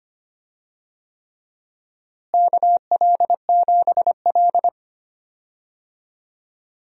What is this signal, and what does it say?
Morse code, heard as a keyed tone, about 25 wpm, 700 Hz. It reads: KL7L
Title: DXCC-100 Call Signs - 4 Characters - Repeated 3x - 25wpm